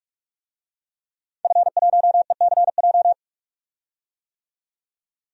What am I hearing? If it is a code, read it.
Morse code, heard as a keyed tone, about 46 wpm, 700 Hz. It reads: F1ECJ
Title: DXCC-100 Call Signs - 5 Characters - 45wpm